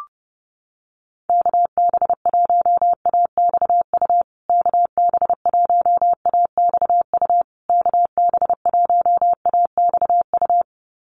Morse code, heard as a keyed tone, about 30 wpm, 700 Hz. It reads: KB1AXU KB1AXU KB1AXU
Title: USA Call Signs Repeated 3x 30wpm